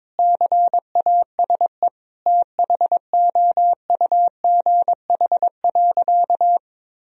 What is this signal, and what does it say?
Morse code, heard as a keyed tone, about 22 wpm, 700 Hz. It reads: CASE THOUGH.
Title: Sentences from Top 1000 Words - Review Part 2 - 22wpm